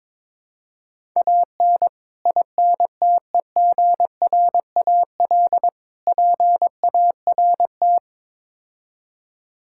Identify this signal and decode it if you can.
Morse code, heard as a keyed tone, about 22 wpm, 700 Hz. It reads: AN INTEGRAL PART
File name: practice / 22wpm / Build A Phrase - 5-Word Phrases - 22wpm